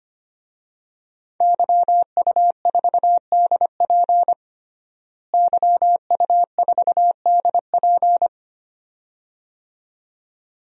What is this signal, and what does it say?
Morse code, heard as a keyed tone, about 25 wpm, 700 Hz. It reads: YU4DP YU4DP
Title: DXCC-100 Call Signs - 5 Characters - Repeated 2x - 25wpm